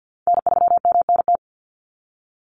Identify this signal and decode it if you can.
Morse code, heard as a keyed tone, about 50 wpm, 700 Hz. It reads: N3MNT
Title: Call Signs Repeated 2x - 50wpm